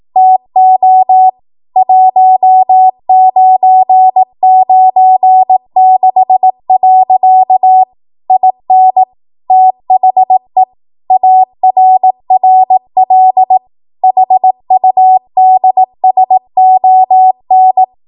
Morse code, heard as a keyed tone, about 18 wpm, 750 Hz. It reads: TO 1996. IN THE ARRL HUDSON